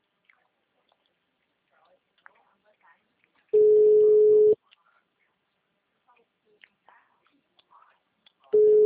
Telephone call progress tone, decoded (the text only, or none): ringback tone